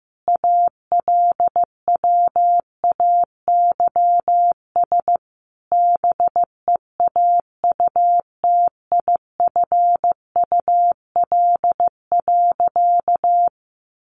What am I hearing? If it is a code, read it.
Morse code, heard as a keyed tone, about 15 wpm, 700 Hz. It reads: ALWAYS BEAUTIFUL.
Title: Sentences from Top 900 Words 15wpm